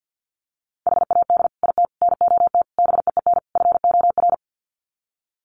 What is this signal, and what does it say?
Morse code, heard as a keyed tone, about 50 wpm, 700 Hz. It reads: HAD IT NOT BEEN FOR
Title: Sets of 5 Words Encore2 50wpm